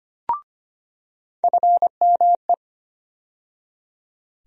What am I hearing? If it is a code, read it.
Morse code, heard as a keyed tone, about 25 wpm, 700 Hz. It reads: FME
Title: USA Call Sign Postfixes - 3 Letters 25wpm